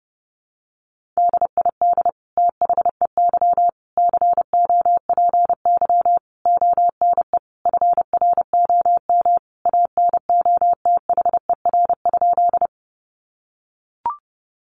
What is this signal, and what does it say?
Morse code, heard as a keyed tone, about 30 wpm, 700 Hz. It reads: DID THEY COPY ONE FROM ANOTHER?
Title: Sentences from Top 1400 Words 30wpm